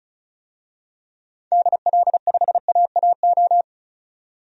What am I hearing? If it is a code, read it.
Morse code, heard as a keyed tone, about 34 wpm, 700 Hz. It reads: DL5AAO
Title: DXCC-100 Call Signs - 6 Characters - 35wpm